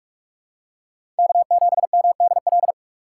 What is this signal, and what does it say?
Morse code, heard as a keyed tone, about 46 wpm, 690 Hz. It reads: K7MDL
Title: USA Call Signs - 5 Characters - 2x Character Spacing - 45wpm